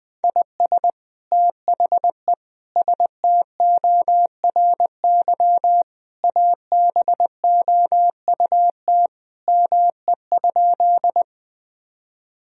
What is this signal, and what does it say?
Morse code, heard as a keyed tone, about 21 wpm, 700 Hz. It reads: IS THE STORY ABOUT ME?
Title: Sentences from Top 1000 Words - Review Part 1 - 20wpm